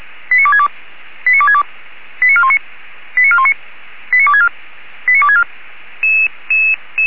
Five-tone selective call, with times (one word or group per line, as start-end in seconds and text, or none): 0.3-0.7 89262
1.3-1.6 89262
2.2-2.6 89419
3.2-3.5 89419
4.1-4.5 89264
5.1-5.4 89264